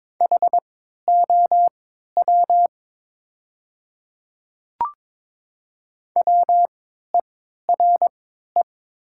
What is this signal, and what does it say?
Morse code, heard as a keyed tone, about 22 wpm, spaced as for 13 wpm, 700 Hz. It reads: HOW WERE
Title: 3x Character Spacing - Top 100 Words - 22wpm